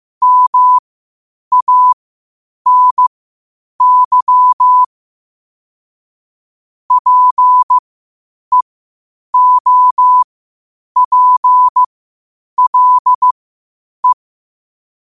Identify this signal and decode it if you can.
Morse code, heard as a keyed tone, about 15 wpm, spaced as for 8 wpm, 1000 Hz. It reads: MANY PEOPLE
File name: qotdfiles / 8wpm